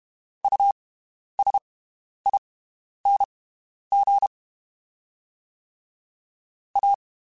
Morse code, heard as a keyed tone, about 32 wpm, spaced as for 11 wpm, 800 Hz. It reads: USING A
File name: (spaced) / QSOs Spaced-mp3s / WD0FBT 1 @30 sp